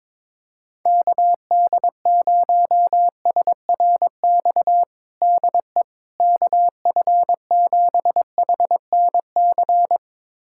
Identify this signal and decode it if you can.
Morse code, heard as a keyed tone, about 22 wpm, 700 Hz. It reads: KD0SRX DE KF7HNC
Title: USA Callsign DE Callsign - 6 Characters - 22wpm